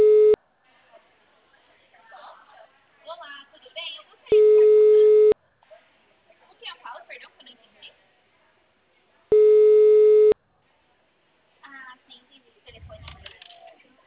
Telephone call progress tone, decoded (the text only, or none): ringback tone